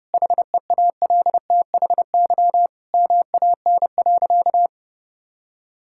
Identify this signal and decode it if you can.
Morse code, heard as a keyed tone, about 30 wpm, 700 Hz. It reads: HEALTHY MAN.
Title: Sentences from Top 1000 Words - Review Part 5 - 30wpm